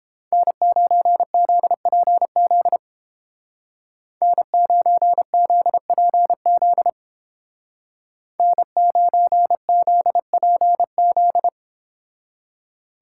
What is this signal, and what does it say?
Morse code, heard as a keyed tone, about 29 wpm, 700 Hz. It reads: N9ZPZ N9ZPZ N9ZPZ